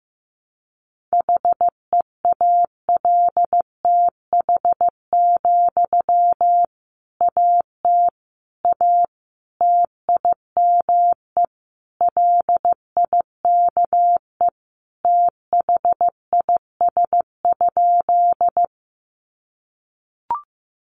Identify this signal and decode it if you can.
Morse code, heard as a keyed tone, about 15 wpm, 700 Hz. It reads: HEALTH, AT A TIME LIKE THIS?
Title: Sentences from Top 600 Words 15wpm